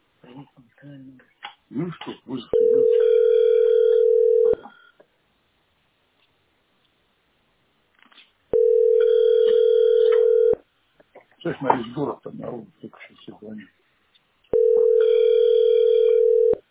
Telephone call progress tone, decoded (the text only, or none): ringback tone